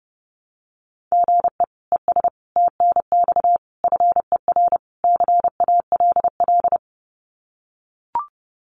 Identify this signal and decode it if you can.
Morse code, heard as a keyed tone, about 30 wpm, 700 Hz. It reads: GE ES TNX FER CALL
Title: Common QSO Phrases Encore 30wpm